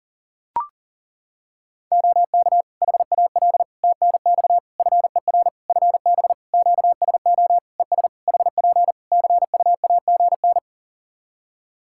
Morse code, heard as a keyed tone, about 40 wpm, 700 Hz. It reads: OK HAL TNX FER FB QSO ES HP CUAGN
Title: Practice QSO - Exchange 4 - Ending - 40wpm